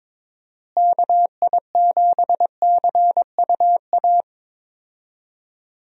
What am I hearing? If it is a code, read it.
Morse code, heard as a keyed tone, about 22 wpm, 700 Hz. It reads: KI7CUA
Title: USA Call Signs - 6 Characters - Repeated 2x - 22wpm